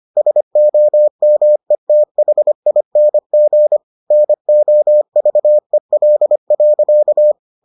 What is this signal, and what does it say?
Morse code, heard as a keyed tone, about 25 wpm, 600 Hz. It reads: SOMETHING NOVEL.